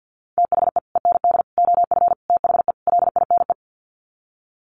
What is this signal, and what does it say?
Morse code, heard as a keyed tone, about 50 wpm, 700 Hz. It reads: THE END OF THE LINE
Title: Sets of 5 Words Encore2 50wpm